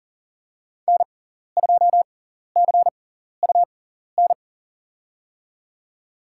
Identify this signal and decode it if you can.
Morse code, heard as a keyed tone, about 40 wpm, spaced as for 14 wpm, 700 Hz. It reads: N2CUN
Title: USA Call Signs - 5 Characters - 6x Character Spacing - 40wpm